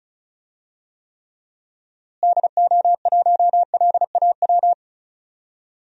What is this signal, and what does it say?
Morse code, heard as a keyed tone, about 34 wpm, 700 Hz. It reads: DO1LAW
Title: DXCC-100 Call Signs - 6 Characters - Repeated 2x - 35wpm